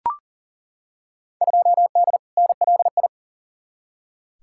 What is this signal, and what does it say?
Morse code, heard as a keyed tone, about 40 wpm, 700 Hz. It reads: 2D NLI
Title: Field Day Exchanges - 40wpm